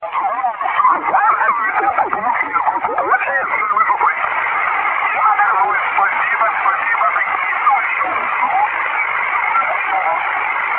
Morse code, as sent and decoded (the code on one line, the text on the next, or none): none